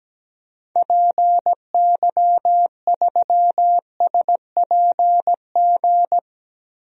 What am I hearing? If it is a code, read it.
Morse code, heard as a keyed tone, about 17 wpm, 700 Hz. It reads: PY3SPG
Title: DXCC-100 Call Signs - 6 Characters - Repeated 2x - 17wpm